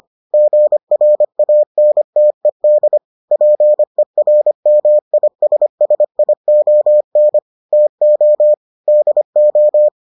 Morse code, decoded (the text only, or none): GRANTED PERMISSION TO DO